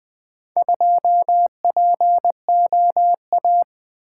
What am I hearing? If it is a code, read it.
Morse code, heard as a keyed tone, about 24 wpm, 700 Hz. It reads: 2POA
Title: Speed-Racing - Four Letter-Number Combinations - 20wpm